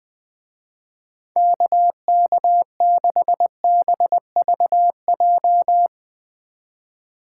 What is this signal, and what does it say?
Morse code, heard as a keyed tone, about 20 wpm, 700 Hz. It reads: KK6BVJ
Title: Speed Racing - USA 2x3 Call Signs 20wpm